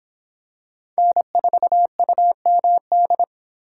Morse code, heard as a keyed tone, about 25 wpm, 700 Hz. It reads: N4UMD